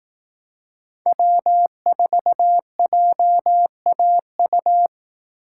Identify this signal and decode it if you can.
Morse code, heard as a keyed tone, about 18 wpm, 700 Hz. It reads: W4JAU
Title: Speed Racing - USA Call Signs 15wpm